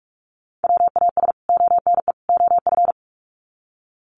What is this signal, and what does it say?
Morse code, heard as a keyed tone, about 44 wpm, 700 Hz. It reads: WAS ONE OF
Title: Sets of 3 Words 45wpm